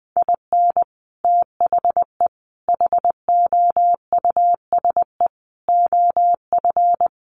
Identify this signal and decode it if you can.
Morse code, heard as a keyed tone, about 20 wpm, 700 Hz. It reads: IN THE HOUSE OF
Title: Sets of 5 Words Encore 20wpm